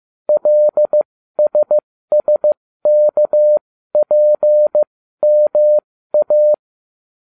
Morse code, heard as a keyed tone, about 15 wpm, spaced as for 13 wpm, 600 Hz. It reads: LSSKPMA